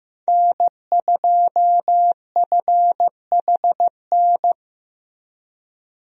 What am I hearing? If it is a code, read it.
Morse code, heard as a keyed tone, about 15 wpm, 700 Hz. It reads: N2FHN
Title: USA Call Signs - 5 Characters - Repeated 2x - 15wpm